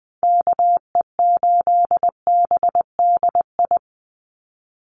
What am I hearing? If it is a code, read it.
Morse code, heard as a keyed tone, about 20 wpm, 700 Hz. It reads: KE8BDI
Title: USA Call Signs 20wpm